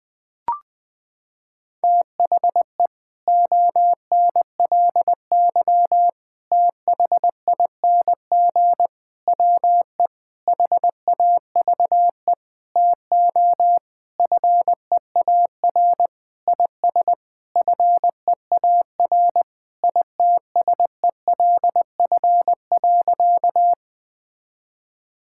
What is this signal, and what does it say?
Morse code, heard as a keyed tone, about 20 wpm, 700 Hz. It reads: THE ONLY THING WE HAVE TO FEAR IS FEAR ITSELF.
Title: Sentences with 9-Word Phrases - 20wpm